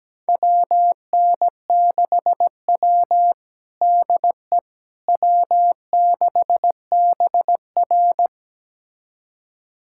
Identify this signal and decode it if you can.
Morse code, heard as a keyed tone, about 17 wpm, 700 Hz. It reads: WN6W DE W6BR
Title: USA Callsign DE Callsign - 4 Characters - 17wpm